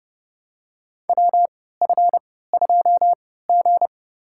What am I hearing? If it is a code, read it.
Morse code, heard as a keyed tone, about 31 wpm, spaced as for 17 wpm, 700 Hz. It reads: WF2G
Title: USA Call Signs - 4 Characters - 3x Character Spacing - 30wpm